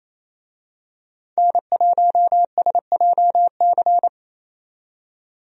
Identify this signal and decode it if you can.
Morse code, heard as a keyed tone, about 28 wpm, 700 Hz. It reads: N1SJC
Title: USA Call Signs - 5 Characters - 28wpm